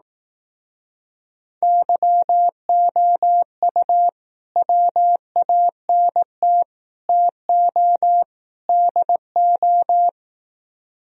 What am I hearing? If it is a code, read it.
Morse code, heard as a keyed tone, about 18 wpm, 700 Hz. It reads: YOU WANT TO DO
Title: Speed-Racing - Sets of 4 Words 15wpm